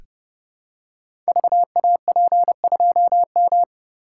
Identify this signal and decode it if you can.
Morse code, heard as a keyed tone, about 30 wpm, 700 Hz. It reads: VAP2M
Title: Code Groups - 5 Letter:Numbers - 30wpm